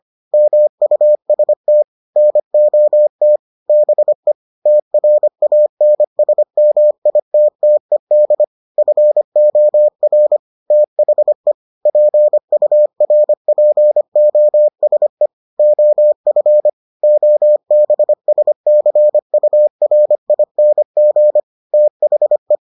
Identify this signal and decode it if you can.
Morse code, heard as a keyed tone, about 25 wpm, 600 Hz. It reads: MUST NOT BE TRANSMITTED FOR THE PURPOSE OF OBSCURING THE